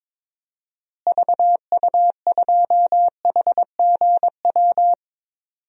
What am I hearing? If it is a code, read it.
Morse code, heard as a keyed tone, about 22 wpm, 700 Hz. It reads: VU2HGW
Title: DXCC-100 Call Signs - 6 Characters - Repeated 2x - 22wpm